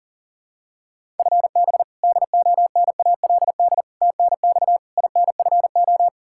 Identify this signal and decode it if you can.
Morse code, heard as a keyed tone, about 40 wpm, 700 Hz. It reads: FB DONALD TNX INFO
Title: Practice QSO - Exchange 3 - More Info - 40wpm